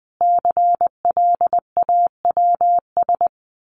Morse code, heard as a keyed tone, about 20 wpm, 700 Hz. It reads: CLAWS
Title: A Sound Of Thunder 20wpm